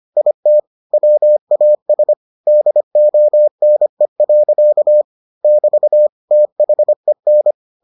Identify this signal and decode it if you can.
Morse code, heard as a keyed tone, about 25 wpm, 600 Hz. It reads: IT WAS DONE. <BT> THEN